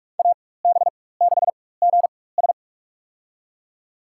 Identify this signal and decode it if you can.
Morse code, heard as a keyed tone, about 43 wpm, spaced as for 21 wpm, 700 Hz. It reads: AB6GS
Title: USA Call Signs - 5 Characters - 4x Character Spacing - 45wpm